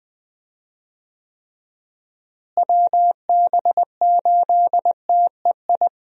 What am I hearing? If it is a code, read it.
Morse code, heard as a keyed tone, about 20 wpm, 700 Hz. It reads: WB8TEI
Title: USA Call Signs - 6 Characters - Repeated 2x - 20wpm